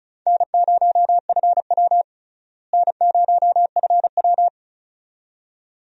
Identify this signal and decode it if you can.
Morse code, heard as a keyed tone, about 36 wpm, 700 Hz. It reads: N0FW N0FW
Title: USA Call Signs - 4 Characters - Repeated 2x - 35wpm